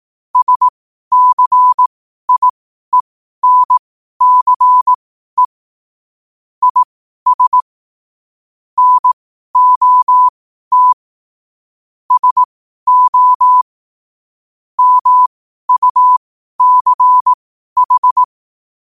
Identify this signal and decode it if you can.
Morse code, heard as a keyed tone, about 18 wpm, spaced as for 12 wpm, 1000 Hz. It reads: SCIENCE IS NOT SO MUCH
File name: qotdfiles / 12wpm